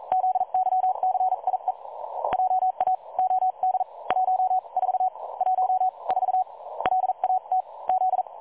Morse code, heard as a keyed tone, about 39 wpm, 750 Hz. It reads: ZYCIE JA OD 14 1 4 LAT Z
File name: dlazaawansowanych / nagraniazpasma / websdr recording 3565.0kHz